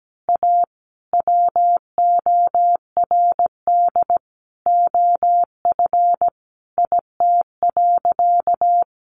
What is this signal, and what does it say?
Morse code, heard as a keyed tone, about 17 wpm, 700 Hz. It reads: A WORD OF IT.
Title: Sentences from Top 1200 Words 17wpm